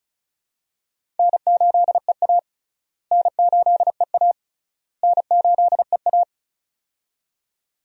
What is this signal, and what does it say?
Morse code, heard as a keyed tone, about 34 wpm, 700 Hz. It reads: N8EA N8EA N8EA
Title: USA Call Signs - 4 Characters - Repeated 3x - 35wpm